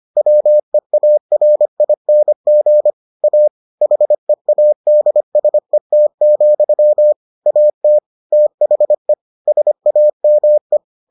Morse code, heard as a keyed tone, about 25 wpm, 600 Hz. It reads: WEARING A HEADSET, AT THE SAME